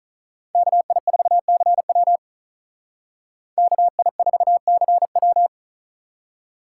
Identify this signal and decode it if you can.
Morse code, heard as a keyed tone, about 46 wpm, 700 Hz. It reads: KI4CW KI4CW